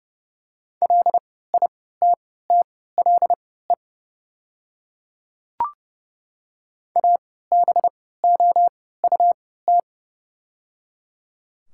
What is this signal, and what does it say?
Morse code, heard as a keyed tone, about 30 wpm, spaced as for 17 wpm, 700 Hz. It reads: LITTLE ABOUT
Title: 3x Character Spacing - Top 300 Words - 30wpm